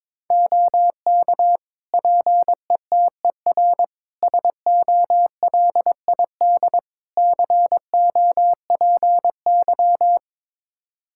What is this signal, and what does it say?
Morse code, heard as a keyed tone, about 22 wpm, 700 Hz. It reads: OK PETER SOLID COPY
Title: K1USN SST QSO Practice 22wpm